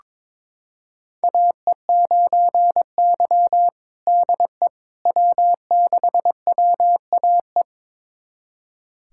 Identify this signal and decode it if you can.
Morse code, heard as a keyed tone, about 22 wpm, 700 Hz. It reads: AE9Y DE W6WAE
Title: USA Callsign DE Callsign 22wpm